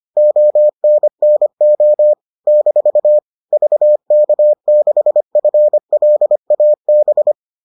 Morse code, heard as a keyed tone, about 25 wpm, 600 Hz. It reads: ONNO - VK6FLAB